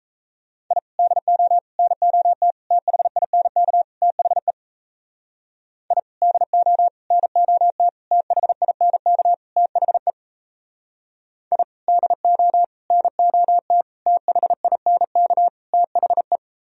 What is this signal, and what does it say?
Morse code, heard as a keyed tone, about 34 wpm, 700 Hz. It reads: I DO NOT THINK THE I DO NOT THINK THE I DO NOT THINK THE
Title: Speed-Racing - Sets of 5 Words 28wpm